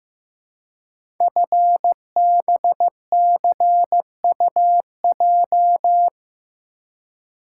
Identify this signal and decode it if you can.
Morse code, heard as a keyed tone, about 15 wpm, 700 Hz. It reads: FBCUJ
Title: Code Groups - 5 Letters - 15wpm